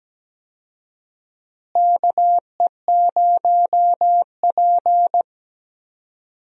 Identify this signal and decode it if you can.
Morse code, heard as a keyed tone, about 17 wpm, 700 Hz. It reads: KE0P
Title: USA Call Signs - 4 Characters - Repeated 2x - 17wpm